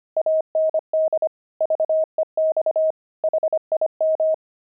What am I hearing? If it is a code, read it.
Morse code, heard as a keyed tone, about 25 wpm, 630 Hz. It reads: AND VEX HIM
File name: pangram-fs-25-cs-25